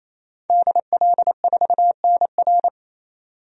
Morse code, heard as a keyed tone, about 28 wpm, 700 Hz. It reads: DL4NR
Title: DXCC-100 Call Signs - 5 Characters - 28wpm